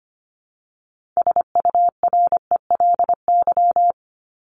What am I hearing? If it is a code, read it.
Morse code, heard as a keyed tone, about 25 wpm, 700 Hz. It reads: SURELY
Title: LY Suffix - Top 100 Words with Base Words 25wpm